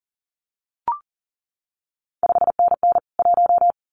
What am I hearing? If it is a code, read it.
Morse code, heard as a keyed tone, about 42 wpm, 700 Hz. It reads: 5NN 1
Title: CQ WW DX Contest Exchange Encore - 40wpm